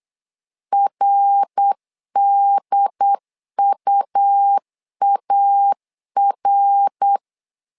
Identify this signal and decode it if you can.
Morse code, heard as a keyed tone, about 8 wpm, 800 Hz. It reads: RDUAR
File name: LEZIONI / 3 MWRO / 3 MWR 8wpm 2444